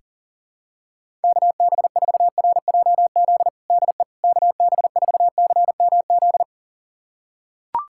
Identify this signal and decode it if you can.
Morse code, heard as a keyed tone, about 40 wpm, 700 Hz. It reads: KB4RJZ DE KB4CMZ
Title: USA Callsign DE Callsign - 6 Characters - 40wpm